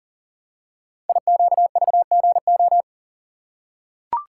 Morse code, heard as a keyed tone, about 40 wpm, 690 Hz. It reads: IQVGO
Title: Code Groups - 5 Letters - 40wpm